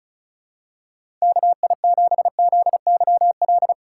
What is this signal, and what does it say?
Morse code, heard as a keyed tone, about 36 wpm, 700 Hz. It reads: KI7ZYL
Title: USA Call Signs - 6 Characters - 2x Character Spacing - 35wpm